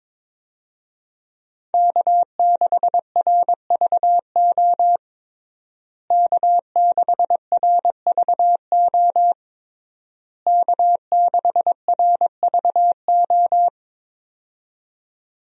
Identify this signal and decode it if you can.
Morse code, heard as a keyed tone, about 22 wpm, 700 Hz. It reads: K6RVO K6RVO K6RVO
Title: USA Call Signs - 5 Characters - Repeated 3x - 22wpm